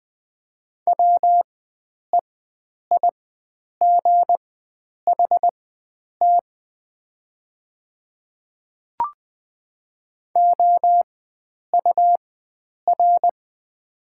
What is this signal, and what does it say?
Morse code, heard as a keyed tone, about 20 wpm, spaced as for 9 wpm, 700 Hz. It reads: WEIGHT OUR
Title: 4x Character Spacing - Top 500 Words - 20wpm